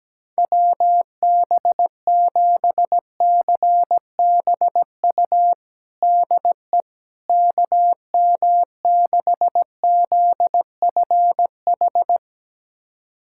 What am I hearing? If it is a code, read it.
Morse code, heard as a keyed tone, about 17 wpm, 700 Hz. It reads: WB7CBU DE KM6ZFH
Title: USA Callsign DE Callsign - 6 Characters - 17wpm